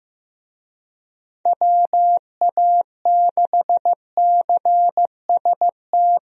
Morse code, heard as a keyed tone, about 15 wpm, 700 Hz. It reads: WA6CST
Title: USA Call Signs - 6 Characters - Repeated 3x - 15wpm